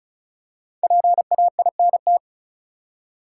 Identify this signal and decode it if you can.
Morse code, heard as a keyed tone, about 36 wpm, 700 Hz. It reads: PAINT
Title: Five Letter Words - 35wpm